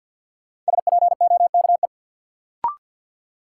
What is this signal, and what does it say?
Morse code, heard as a keyed tone, about 47 wpm, 700 Hz. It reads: SPOKE
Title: Sentences from Top 1200 Words - New Word Drill - 50wpm